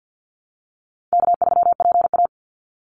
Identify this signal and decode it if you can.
Morse code, heard as a keyed tone, about 50 wpm, 700 Hz. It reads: K3PA
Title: Easy Call Signs Repeated 3x - 50wpm